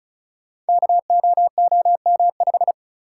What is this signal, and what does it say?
Morse code, heard as a keyed tone, about 36 wpm, 700 Hz. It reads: KOOM5
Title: Code Groups - 5 Letter:Numbers - 35wpm